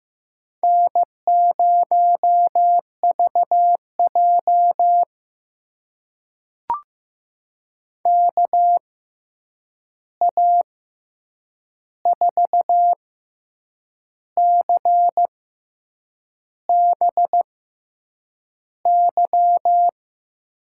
Morse code, heard as a keyed tone, about 15 wpm, spaced as for 7 wpm, 700 Hz. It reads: N0VJ KA4CBY
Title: USA Call Signs - 6x Character Spacing - 15wpm